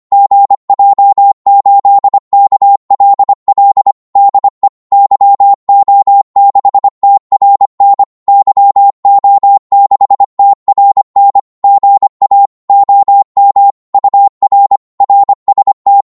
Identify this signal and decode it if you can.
Morse code, heard as a keyed tone, about 25 wpm, 800 Hz. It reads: GJ8KLL DE YO6TRN YO6TRN GA OM UR RST
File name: Morse / qso-25wpm